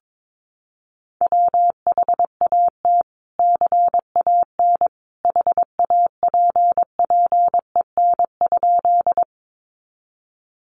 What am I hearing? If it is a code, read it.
Morse code, heard as a keyed tone, about 22 wpm, 700 Hz. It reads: WHAT CAN HAPPEN?
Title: Sentences from Top 400 Words 22wpm